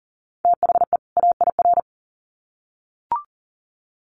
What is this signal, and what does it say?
Morse code, heard as a keyed tone, about 40 wpm, 700 Hz. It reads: THE AIR
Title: Sets of 2 Words 40wpm